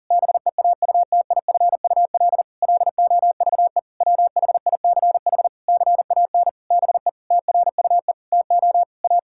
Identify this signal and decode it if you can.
Morse code, heard as a keyed tone, about 37 wpm, 700 Hz. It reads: BEAUTIFUL LOVE WHICH CAN BE TRUE TO A
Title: Sentences from Top 1600 Words 40wpm